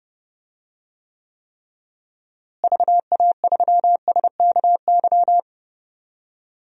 Morse code, heard as a keyed tone, about 30 wpm, 700 Hz. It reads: VA3SKY
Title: DXCC-100 Call Signs - 6 Characters - Repeated 3x - 30wpm